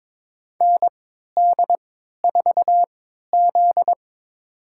Morse code, heard as a keyed tone, about 22 wpm, spaced as for 12 wpm, 700 Hz. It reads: ND4Z
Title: USA Call Signs - 4 Characters - 3x Character Spacing - 22wpm